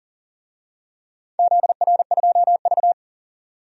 Morse code, heard as a keyed tone, about 40 wpm, 700 Hz. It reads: ZR2V
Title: DXCC-100 Call Signs - 4 Characters - 40wpm